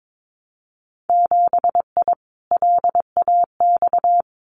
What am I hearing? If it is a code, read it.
Morse code, heard as a keyed tone, about 22 wpm, 700 Hz. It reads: 7I LAX
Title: Winter Field Day Exchanges Encore 22wpm